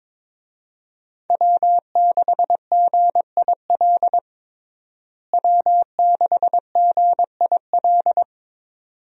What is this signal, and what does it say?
Morse code, heard as a keyed tone, about 22 wpm, 700 Hz. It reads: W6GIL W6GIL
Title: USA Call Signs - 5 Characters - Repeated 2x - 22wpm